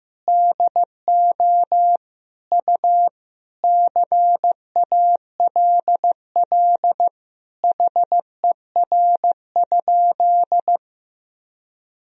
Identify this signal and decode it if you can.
Morse code, heard as a keyed tone, about 15 wpm, 700 Hz. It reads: DO U CALL HER?
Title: Sentences from Top 100 Words and Abbreviations 15wpm